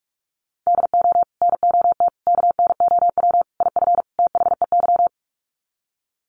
Morse code, heard as a keyed tone, about 44 wpm, 700 Hz. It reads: DO NOT KNOW IF THEY
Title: Sets of 5 Words Encore 45wpm